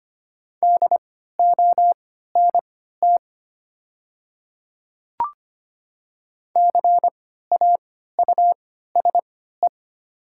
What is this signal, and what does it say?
Morse code, heard as a keyed tone, about 25 wpm, spaced as for 14 wpm, 700 Hz. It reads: DONT CAUSE
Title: 3x Character Spacing - Top 300 Words - 25wpm